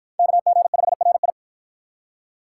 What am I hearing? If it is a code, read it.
Morse code, heard as a keyed tone, about 53 wpm, 700 Hz. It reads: KG5RI